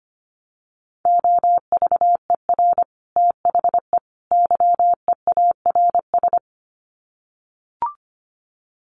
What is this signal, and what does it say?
Morse code, heard as a keyed tone, about 25 wpm, 700 Hz. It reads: OVER THE YEARS
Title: Sets of 3 Words 25wpm